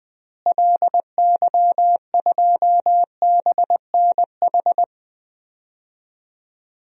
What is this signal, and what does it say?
Morse code, heard as a keyed tone, about 21 wpm, 700 Hz. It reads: LY2BNH
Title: DXCC-100 Call Signs - 6 Characters - Repeated 3x - 20wpm